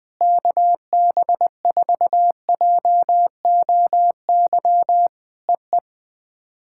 Morse code, heard as a keyed tone, about 19 wpm, 700 Hz. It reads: KB4JOY EE
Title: POTA QSO Practice - 20wpm